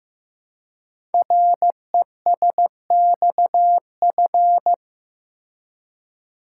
Morse code, heard as a keyed tone, about 15 wpm, 700 Hz. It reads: RESXF
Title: Code Groups - 5 Letters - 15wpm